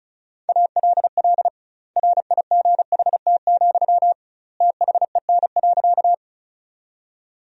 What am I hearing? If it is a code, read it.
Morse code, heard as a keyed tone, about 34 wpm, 700 Hz. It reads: ALL RIGHT, THEN.
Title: 2x Word-Spacing - Sentences from Top 200 Words - 35wpm